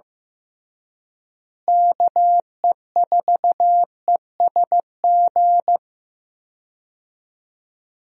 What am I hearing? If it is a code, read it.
Morse code, heard as a keyed tone, about 15 wpm, 700 Hz. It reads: KE4ESG
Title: USA Call Signs - 6 Characters - Repeated 3x - 15wpm